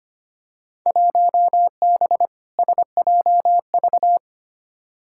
Morse code, heard as a keyed tone, about 25 wpm, 700 Hz. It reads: 1B SJV
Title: Field Day Exchanges - 25wpm